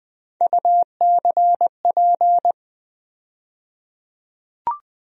Morse code, heard as a keyed tone, about 21 wpm, 700 Hz. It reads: UCP
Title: USA Call Sign Postfixes - 3 Letters 20wpm